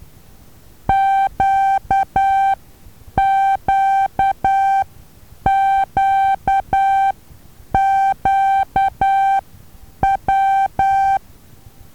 Morse code, heard as a keyed tone, about 12 wpm, spaced as for 8 wpm, 780 Hz. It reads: QQQQW
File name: Taviro HA6VA B